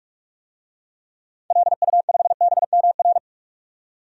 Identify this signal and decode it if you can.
Morse code, heard as a keyed tone, about 45 wpm, 700 Hz. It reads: LU5BMR
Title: DXCC-100 Call Signs - 6 Characters - 45wpm